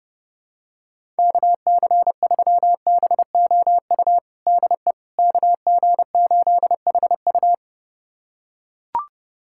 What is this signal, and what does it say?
Morse code, heard as a keyed tone, about 30 wpm, 690 Hz. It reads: KC3BOU DE KG8HU
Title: USA Callsign DE Callsign 30wpm